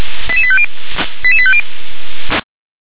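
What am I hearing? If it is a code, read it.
Five-tone selective call (ZVEI), heard as quick runs of five tones, tones of about 70 ms. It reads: 88850 88850
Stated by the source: ZVEI five-tone call received by SDR.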